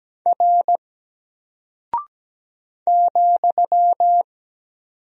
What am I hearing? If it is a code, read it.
Morse code, heard as a keyed tone, about 17 wpm, 700 Hz. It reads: R ,
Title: Single Character (Letter:Number:Punctuation:Prosign) Alt 17wpm